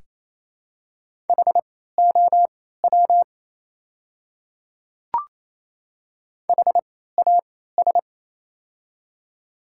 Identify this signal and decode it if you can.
Morse code, heard as a keyed tone, about 28 wpm, spaced as for 16 wpm, 700 Hz. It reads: HOW HAS
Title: 3x Character Spacing - Top 100 Words - 28wpm